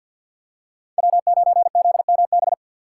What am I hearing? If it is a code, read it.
Morse code, heard as a keyed tone, about 53 wpm, 700 Hz. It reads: W9ZMB
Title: USA Call Signs - 4x Character Spacing - 50wpm